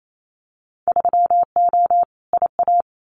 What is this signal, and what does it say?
Morse code, heard as a keyed tone, about 28 wpm, 700 Hz. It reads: 3O IA
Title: Winter Field Day Exchanges 28wpm